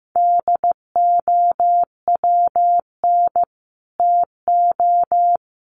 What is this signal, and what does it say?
Morse code, heard as a keyed tone, about 15 wpm, 700 Hz. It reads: DOWN TO
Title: Sets of 3 Words 15wpm